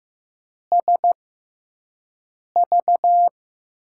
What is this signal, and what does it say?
Morse code, heard as a keyed tone, about 15 wpm, spaced as for 5 wpm, 700 Hz. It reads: SV
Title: USA Call Signs - 6x Character Spacing - 15wpm